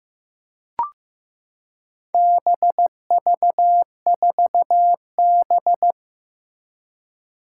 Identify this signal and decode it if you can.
Morse code, heard as a keyed tone, about 15 wpm, 700 Hz. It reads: BV4B
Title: Four Challenging Letter-Number Combinations 15wpm